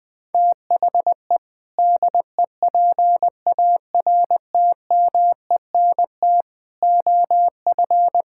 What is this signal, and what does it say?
Morse code, heard as a keyed tone, about 20 wpm, 700 Hz. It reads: THE DEPARTMENT OF
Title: Sentences from Top 1000 Words - Review Part 5 - 20wpm